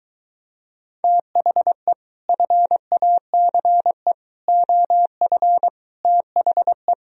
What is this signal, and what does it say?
Morse code, heard as a keyed tone, about 24 wpm, 700 Hz. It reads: THE FACE OF THE